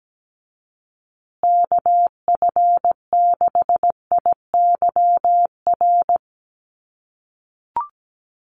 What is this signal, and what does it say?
Morse code, heard as a keyed tone, about 17 wpm, 700 Hz. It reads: KF6IYR
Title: USA Call Signs Repeated 3x 17wpm